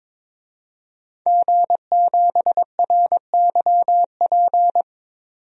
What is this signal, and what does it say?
Morse code, heard as a keyed tone, about 22 wpm, 700 Hz. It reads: G7RYP
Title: DXCC-100 Call Signs - 5 Characters - 22wpm